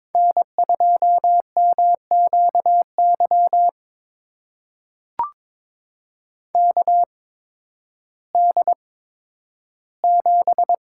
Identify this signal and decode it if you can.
Morse code, heard as a keyed tone, about 22 wpm, spaced as for 12 wpm, 700 Hz. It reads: N2MQY KD7
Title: USA Call Signs - 5 Characters - 8x Character Spacing - 22wpm